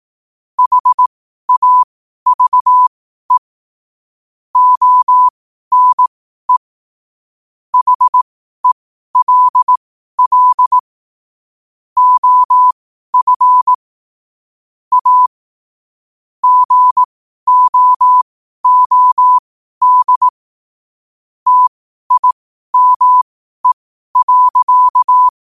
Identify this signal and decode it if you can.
Morse code, heard as a keyed tone, about 18 wpm, spaced as for 12 wpm, 1000 Hz. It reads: HAVE ONE HELL OF A GOOD TIME.
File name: qotdfiles / 12wpm